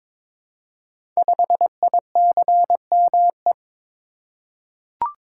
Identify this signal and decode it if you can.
Morse code, heard as a keyed tone, about 22 wpm, 700 Hz. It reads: 5ICME
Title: Code Groups - 5 Letter:Numbers - 22wpm